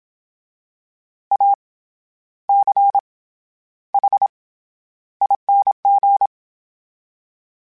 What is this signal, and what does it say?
Morse code, heard as a keyed tone, about 27 wpm, 800 Hz. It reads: A C H ING
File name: Special INGs-mp3s / Special INGs-mp3s / SlowToFaster ing @25